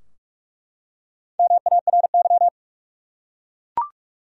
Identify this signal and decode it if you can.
Morse code, heard as a keyed tone, about 43 wpm, 700 Hz. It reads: MARY
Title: USA Common Names - Max 4 Letters - 45wpm